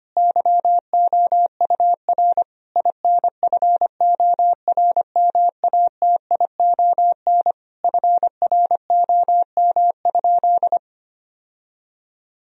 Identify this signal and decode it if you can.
Morse code, heard as a keyed tone, about 25 wpm, 700 Hz. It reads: YOUR INFORMATION FROM?
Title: Sentences from Top 1000 Words - Review Part 3 - 25wpm